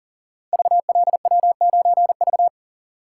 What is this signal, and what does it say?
Morse code, heard as a keyed tone, about 38 wpm, 700 Hz. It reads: VLW9V
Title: Code Groups - 5 Letter:Numbers - 40wpm